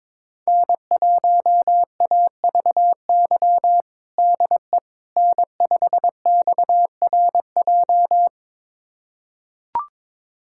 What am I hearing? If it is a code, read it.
Morse code, heard as a keyed tone, about 22 wpm, 700 Hz. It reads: N1AVY DE N5XRJ
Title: USA Callsign DE Callsign - 5 Characters - 22wpm